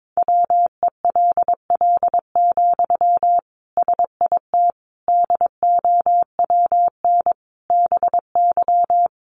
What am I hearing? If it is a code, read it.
Morse code, heard as a keyed tone, about 22 wpm, 700 Hz. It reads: WELL, SIT DOWN BY
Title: Sentences from Top 500 Words 22wpm